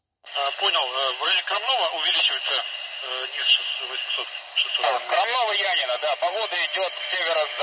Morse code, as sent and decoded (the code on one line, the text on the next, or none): none